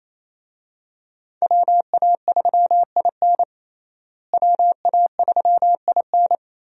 Morse code, heard as a keyed tone, about 28 wpm, 700 Hz. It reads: WA3IN WA3IN
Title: USA Call Signs - 5 Characters - Repeated 2x - 28wpm